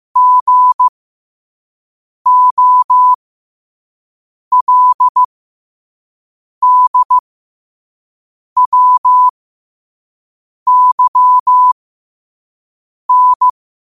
Morse code, heard as a keyed tone, about 15 wpm, spaced as for 5 wpm, 1000 Hz. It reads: GOLDWYN
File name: qotdfiles / 5wpm